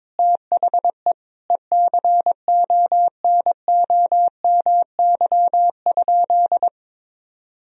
Morse code, heard as a keyed tone, about 22 wpm, 700 Hz. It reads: THE ECONOMY?
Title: Sentences from Top 1000 Words - Review Part 3 - 22wpm